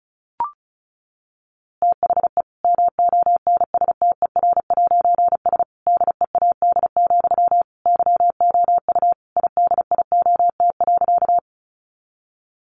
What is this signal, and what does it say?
Morse code, heard as a keyed tone, about 34 wpm, 700 Hz. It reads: THE MONSTER'S DEAD, YOU IDIOT.
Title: A Sound Of Thunder 35wpm